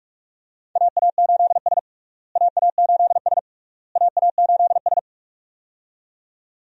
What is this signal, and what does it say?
Morse code, heard as a keyed tone, about 48 wpm, 700 Hz. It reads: AA8S AA8S AA8S
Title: USA Call Signs - 4 Characters - Repeated 3x - 45wpm